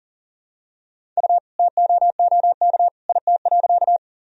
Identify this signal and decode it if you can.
Morse code, heard as a keyed tone, about 40 wpm, 700 Hz. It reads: U TOOK IT.
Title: Sentences from Top 300 Words and Abbreviations 40wpm